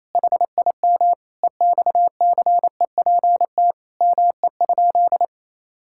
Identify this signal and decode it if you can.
Morse code, heard as a keyed tone, about 28 wpm, 700 Hz. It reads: HIM EXCEPT ME?
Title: Sentences from Top 1600 Words 28wpm